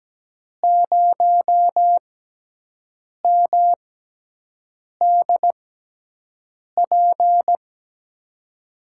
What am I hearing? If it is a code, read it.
Morse code, heard as a keyed tone, about 17 wpm, spaced as for 6 wpm, 700 Hz. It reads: 0MDP
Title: USA Call Signs - 5 Characters - 6x Character Spacing - 17wpm